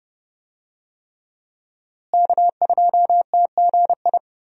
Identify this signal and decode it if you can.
Morse code, heard as a keyed tone, about 30 wpm, 700 Hz. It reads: K2TGI
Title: USA Call Signs - 5 Characters - Repeated 3x - 30wpm